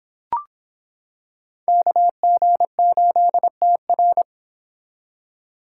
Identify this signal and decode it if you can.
Morse code, heard as a keyed tone, about 26 wpm, 700 Hz. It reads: KG8TR